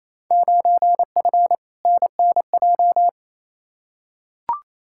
Morse code, heard as a keyed tone, about 28 wpm, 700 Hz. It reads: 9F NNJ
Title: Field Day Exchanges Encore - 28wpm